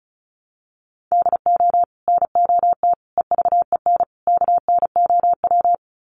Morse code, heard as a keyed tone, about 34 wpm, 700 Hz. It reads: DO NOT EVEN KNOW
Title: Sets of 4 Words 35wpm